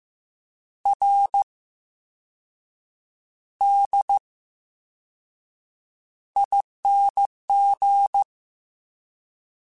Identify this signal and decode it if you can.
Morse code, heard as a keyed tone, about 15 wpm, 800 Hz. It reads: R D ING
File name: Special INGs / Special INGs-mp3s / Slow ing @13